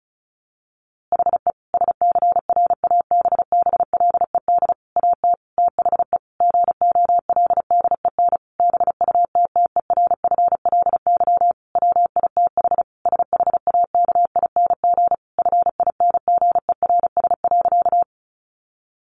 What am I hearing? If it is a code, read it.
Morse code, heard as a keyed tone, about 34 wpm, 700 Hz. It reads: HE SCRABBLED AT THE GOLDEN BUTTERFLY WITH SHAKING FINGERS.
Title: A Sound Of Thunder 35wpm